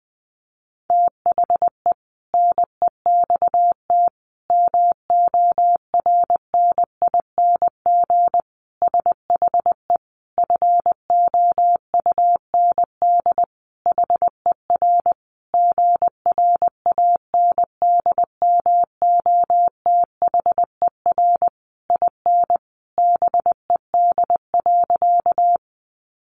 Morse code, heard as a keyed tone, about 20 wpm, 700 Hz. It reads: THE NEXT MORNING SHE FOUND HER GRANDMOTHER IN BED.
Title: Sentences from Top 1200 Words 20wpm